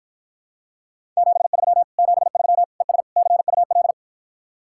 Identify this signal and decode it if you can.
Morse code, heard as a keyed tone, about 54 wpm, 700 Hz. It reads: 73 73 ES CUL